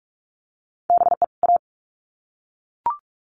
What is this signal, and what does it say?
Morse code, heard as a keyed tone, about 44 wpm, 700 Hz. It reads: BE A
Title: Sets of 2 Words 45wpm